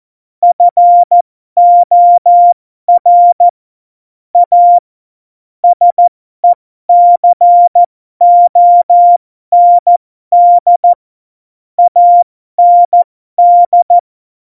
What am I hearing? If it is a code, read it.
Morse code, heard as a keyed tone, about 14 wpm, spaced as for 12 wpm, 700 Hz. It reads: FOR A SECOND AND
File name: Morsekurs / Witze